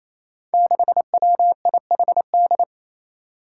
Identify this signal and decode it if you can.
Morse code, heard as a keyed tone, about 29 wpm, 700 Hz. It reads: 6WIHD
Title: Five Challenging Letter-Number Combinations 28wpm